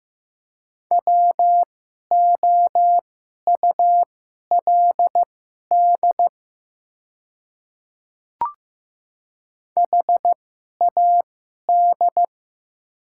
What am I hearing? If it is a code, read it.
Morse code, heard as a keyed tone, about 15 wpm, spaced as for 11 wpm, 700 Hz. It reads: WOULD HAD
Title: 2x Character Spacing - Top 100 Words - 15wpm